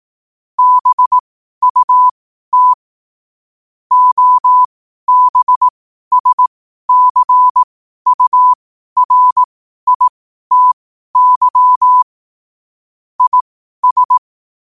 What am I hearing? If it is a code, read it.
Morse code, heard as a keyed tone, about 18 wpm, spaced as for 12 wpm, 1000 Hz. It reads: BUT OBSCURITY IS
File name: qotdfiles / 12wpm